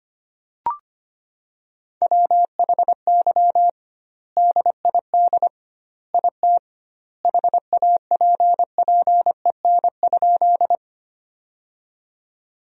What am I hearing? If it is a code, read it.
Morse code, heard as a keyed tone, about 25 wpm, 700 Hz. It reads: WHY DID IT HAPPEN?
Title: 2x Word-Spacing - Sentences from Top 400 Words - 25wpm